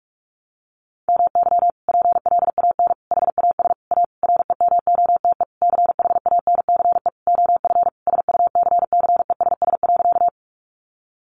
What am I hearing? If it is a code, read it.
Morse code, heard as a keyed tone, about 46 wpm, 700 Hz. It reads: MY PLAN HAS A REMOTE CHANCE OF SUCCESS.
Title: Sentences from Top 1300 Words 45wpm